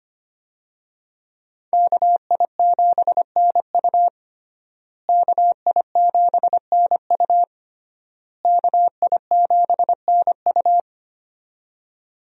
Morse code, heard as a keyed tone, about 25 wpm, 700 Hz. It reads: KI7NU KI7NU KI7NU
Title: USA Call Signs - 5 Characters - Repeated 3x - 25wpm